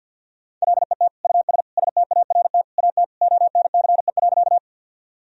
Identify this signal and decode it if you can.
Morse code, heard as a keyed tone, about 46 wpm, 700 Hz. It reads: LET US START AT ONCE.
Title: Sentences from Top 1000 Words - Review Part 1 - 50wpm